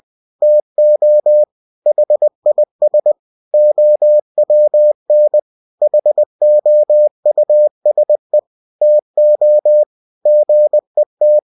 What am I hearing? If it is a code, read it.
Morse code, heard as a keyed tone, about 20 wpm, 600 Hz. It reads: TO HIS OWN HOUSE TO GET